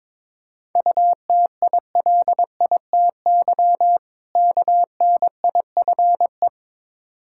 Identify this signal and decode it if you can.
Morse code, heard as a keyed tone, about 21 wpm, 690 Hz. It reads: UTILITY KNIFE
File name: Difficult Words with Hints - 22wpm